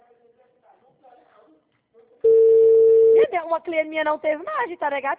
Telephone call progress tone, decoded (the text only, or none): ringback tone